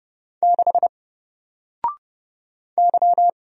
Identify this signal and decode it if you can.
Morse code, heard as a keyed tone, about 30 wpm, 700 Hz. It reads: 6 Y
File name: Single-Letter-Number-30wpm